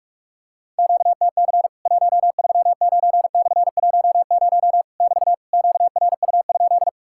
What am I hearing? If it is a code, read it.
Morse code, heard as a keyed tone, about 44 wpm, 700 Hz. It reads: QTC 139/10 <BT> QRU?
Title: Worked All Europe DX Contest - QTC Exchange - 45wpm